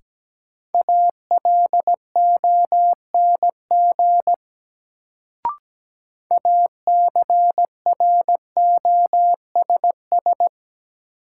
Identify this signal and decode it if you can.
Morse code, heard as a keyed tone, about 17 wpm, 700 Hz. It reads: ALONG ACROSS
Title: Sentences from Top 700 Words - New Word Drill - 17wpm